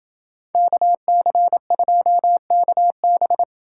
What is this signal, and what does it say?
Morse code, heard as a keyed tone, about 28 wpm, 700 Hz. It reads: KC2KB